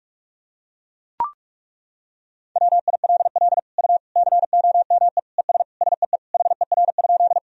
Morse code, heard as a keyed tone, about 41 wpm, 700 Hz. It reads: WILL U COME ES SEE HER?
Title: Sentences from Top 100 Words and Abbreviations 45wpm